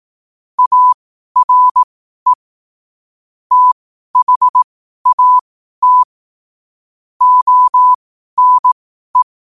Morse code, heard as a keyed tone, about 18 wpm, spaced as for 12 wpm, 1000 Hz. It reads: ARE THAT ONE
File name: qotdfiles / 12wpm